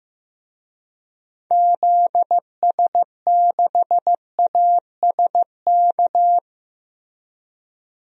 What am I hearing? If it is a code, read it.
Morse code, heard as a keyed tone, about 15 wpm, 700 Hz. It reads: ZS6ASK
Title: DXCC-100 Call Signs - 6 Characters - Repeated 3x - 15wpm